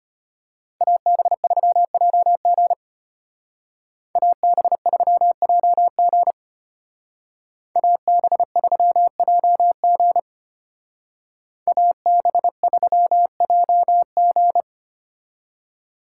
Morse code, heard as a keyed tone, about 33 wpm, 700 Hz. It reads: AB3JG AB3JG AB3JG AB3JG